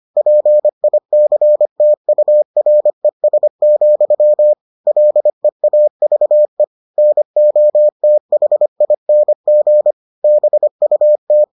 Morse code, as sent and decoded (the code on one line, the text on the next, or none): .--. .. -.-. - ..- .-. ... --..-- / .-.. .- ...- . / -. --- - .... .. -. --. / -... ..- -
PICTURES, LEAVE NOTHING BUT